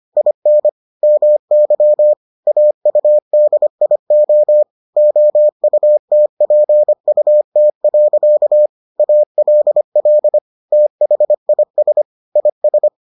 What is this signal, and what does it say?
Morse code, heard as a keyed tone, about 25 wpm, 600 Hz. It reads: IN MY AUDIO OUTPUT. ALL THIS IS